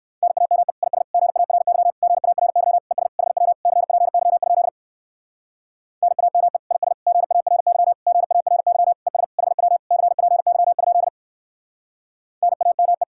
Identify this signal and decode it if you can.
Morse code, heard as a keyed tone, about 58 wpm, 700 Hz. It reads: NAME IS GARY GARY ES HW CPY? NAME IS GARY GARY ES HW CPY? NAME